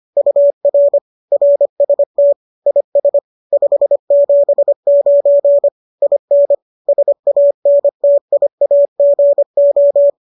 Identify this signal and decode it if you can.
Morse code, heard as a keyed tone, about 25 wpm, 570 Hz. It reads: UR RST IS 579 IN SANTIAGO